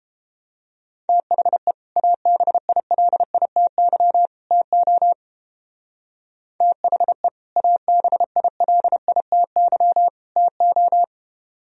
Morse code, heard as a keyed tone, about 31 wpm, 700 Hz. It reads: THE ABILITY TO THE ABILITY TO